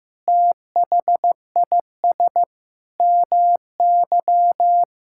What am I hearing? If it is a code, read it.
Morse code, heard as a keyed tone, about 15 wpm, 700 Hz. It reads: THIS MY
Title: Sentences from Top 1000 Words - Review Part 5 - 15wpm